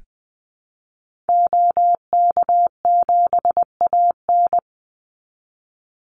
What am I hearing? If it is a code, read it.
Morse code, heard as a keyed tone, about 20 wpm, 700 Hz. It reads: OK7AN
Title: Call Signs Repeated 3x - 20wpm